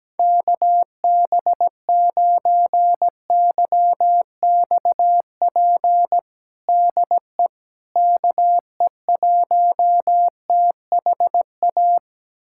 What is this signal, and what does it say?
Morse code, heard as a keyed tone, about 17 wpm, 700 Hz. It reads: KB9YXP DE KE1THA
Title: USA Callsign DE Callsign 17wpm